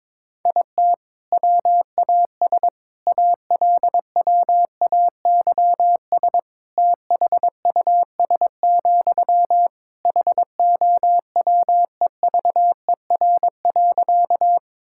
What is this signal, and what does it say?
Morse code, heard as a keyed tone, about 22 wpm, 700 Hz. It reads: IT WAS ALWAYS THUS, HOWEVER.
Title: Sentences from Top 1000 Words - Review Part 5 - 22wpm